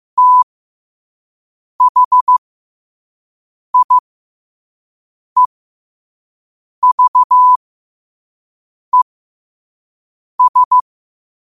Morse code, heard as a keyed tone, about 15 wpm, spaced as for 5 wpm, 1000 Hz. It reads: THIEVES
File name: qotdfiles / 5wpm